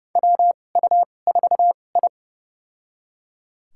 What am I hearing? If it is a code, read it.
Morse code, heard as a keyed tone, about 30 wpm, spaced as for 22 wpm, 700 Hz. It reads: WU4I
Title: USA Call Signs - 2x Character Spacing - 30wpm